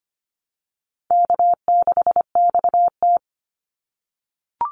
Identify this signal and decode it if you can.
Morse code, heard as a keyed tone, about 25 wpm, 700 Hz. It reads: K6XT
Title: Easy Call Signs - 25wpm